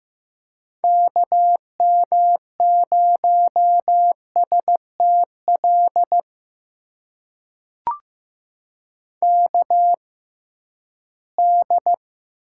Morse code, heard as a keyed tone, about 15 wpm, spaced as for 11 wpm, 700 Hz. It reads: KM0STL KD
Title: USA Call Signs - 6 Characters - 6x Character Spacing - 15wpm